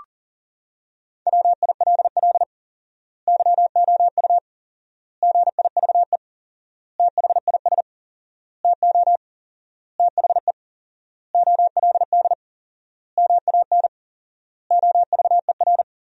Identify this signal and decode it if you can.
Morse code, heard as a keyed tone, about 40 wpm, 700 Hz. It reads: WILL YOU GIVE THIS TO THE OLD MAN OVER
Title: 4x Word-Spacing - Sentences from Top 200 Words - 40wpm